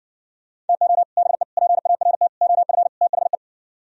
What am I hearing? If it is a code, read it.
Morse code, heard as a keyed tone, about 69 wpm, 700 Hz. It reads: TO BE PART OF THE